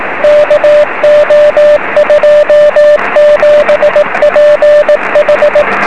Morse code, heard as a keyed tone, about 18 wpm, 590 Hz, with radio noise. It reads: KO27PH
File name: 0809z yl2cp 23cm